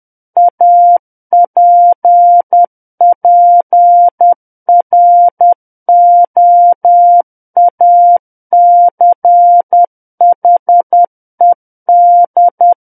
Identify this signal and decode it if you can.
Morse code, heard as a keyed tone, about 10 wpm, 700 Hz. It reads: APPROACHED
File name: Book One / 01-Chapter 1 - 10 WPM